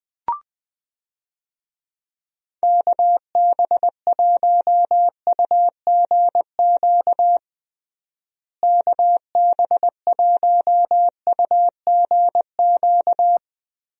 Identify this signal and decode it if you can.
Morse code, heard as a keyed tone, about 20 wpm, 700 Hz. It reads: KB1UGQ KB1UGQ
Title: USA Call Signs - 6 Characters - Repeated 3x - 20wpm